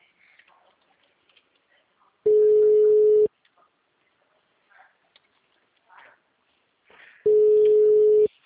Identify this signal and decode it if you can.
Ringback tone (Europe, 425 Hz).
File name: out- -2010- - - .0